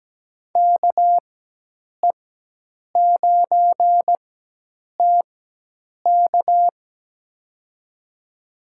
Morse code, heard as a keyed tone, about 17 wpm, spaced as for 8 wpm, 700 Hz. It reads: KE9TK
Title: USA Call Signs - 5 Characters - 4x Character Spacing - 17wpm